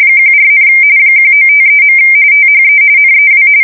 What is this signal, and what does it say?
RTTY, about 45.45 baud. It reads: 14, 1953. THE NEW
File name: fsk170 / USS-New-Jersey